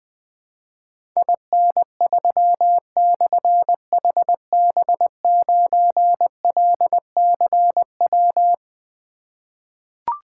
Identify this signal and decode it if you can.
Morse code, heard as a keyed tone, about 20 wpm, 700 Hz. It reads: IN3/HB9LCW
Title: Difficult Call Signs 20wpm